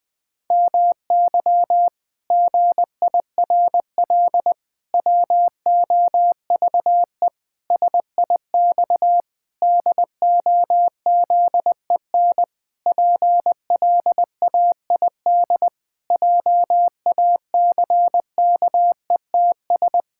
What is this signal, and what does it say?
Morse code, heard as a keyed tone, about 20 wpm, 700 Hz. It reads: MY GIRL WOVE SIX DOZEN PLAID JACKETS
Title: Pangram Challenge v2 - 20wpm